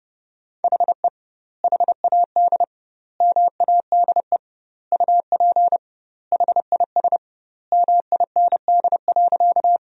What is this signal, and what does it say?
Morse code, heard as a keyed tone, about 30 wpm, 700 Hz. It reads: HE HAD MADE UP HIS MIND.
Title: 2x Word-Spacing - Sentences from Top 500 Words - 30wpm